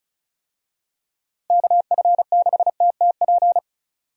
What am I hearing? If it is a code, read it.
Morse code, heard as a keyed tone, about 36 wpm, 700 Hz. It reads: KF6TTP
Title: USA Call Signs - 6 Characters - Repeated 3x - 35wpm